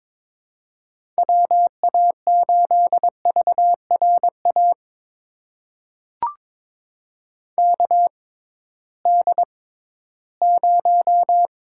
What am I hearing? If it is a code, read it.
Morse code, heard as a keyed tone, about 22 wpm, spaced as for 14 wpm, 700 Hz. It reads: WA8VRA KD0
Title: USA Call Signs - 6 Characters - 6x Character Spacing - 22wpm